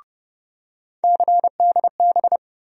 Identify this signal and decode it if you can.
Morse code, heard as a keyed tone, about 30 wpm, 700 Hz. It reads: CDB
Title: USA Call Sign Postfixes - 3 Letters 30wpm